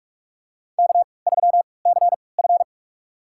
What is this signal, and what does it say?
Morse code, heard as a keyed tone, about 41 wpm, spaced as for 25 wpm, 700 Hz. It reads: K3CF
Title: USA Call Signs - 4 Characters - 3x Character Spacing - 45wpm